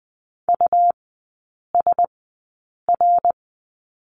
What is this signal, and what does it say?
Morse code, heard as a keyed tone, about 20 wpm, spaced as for 8 wpm, 700 Hz. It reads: USR
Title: 100 Lesson 17e - Letter u - Three Characters 20wpm